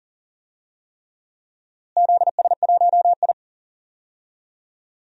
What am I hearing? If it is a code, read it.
Morse code, heard as a keyed tone, about 40 wpm, 700 Hz. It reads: ZS1I
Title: DXCC-100 Call Signs - 4 Characters - Repeated 3x - 40wpm